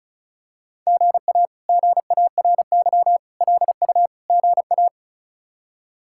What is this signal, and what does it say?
Morse code, heard as a keyed tone, about 34 wpm, 700 Hz. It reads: GA GARY LU GA
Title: K1USN SST QSO Practice - 35wpm